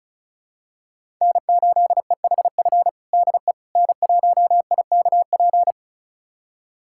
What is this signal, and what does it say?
Morse code, heard as a keyed tone, about 34 wpm, 700 Hz. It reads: N8EHF DE N1IKP
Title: USA Callsign DE Callsign - 5 Characters - 35wpm